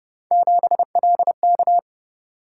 Morse code, heard as a keyed tone, about 29 wpm, 700 Hz. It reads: 7LK